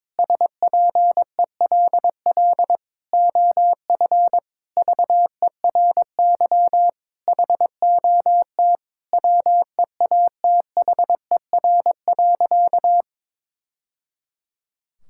Morse code, heard as a keyed tone, about 22 wpm, 700 Hz. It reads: SPELL OF VERY HOT WEATHER.
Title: Sentences from Top 1000 Words - Review Part 5 - 22wpm